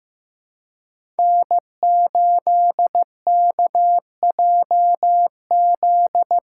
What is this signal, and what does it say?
Morse code, heard as a keyed tone, about 15 wpm, 700 Hz. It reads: N8KJZ
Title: Code Groups - 5 Letter:Numbers - 15wpm